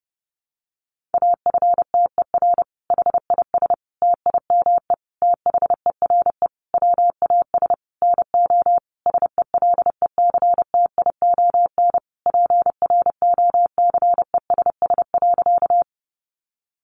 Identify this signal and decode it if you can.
Morse code, heard as a keyed tone, about 30 wpm, 700 Hz. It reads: AFTER HIS TIME THERE WAS NO SELECTION PROCESS.
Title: Sentences from Top 1400 Words 30wpm